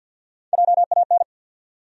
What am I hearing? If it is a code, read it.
Morse code, heard as a keyed tone, about 52 wpm, 700 Hz. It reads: JAN
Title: USA Common Names - Max 5 Letters - 50wpm